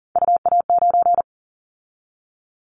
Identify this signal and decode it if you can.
Morse code, heard as a keyed tone, about 40 wpm, 700 Hz. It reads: UA9
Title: Callsign Prefixes for Top 50 DXCC - 40wpm